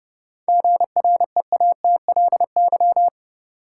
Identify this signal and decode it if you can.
Morse code, heard as a keyed tone, about 32 wpm, 700 Hz. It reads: GREATLY
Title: Sentences from Top 1400 Words - New Word Drill - 30wpm